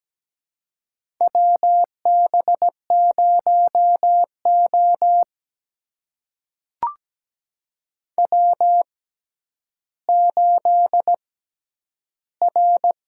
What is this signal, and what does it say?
Morse code, heard as a keyed tone, about 17 wpm, spaced as for 10 wpm, 700 Hz. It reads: WB0O W8R
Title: USA Call Signs - 4 Characters - 6x Character Spacing - 17wpm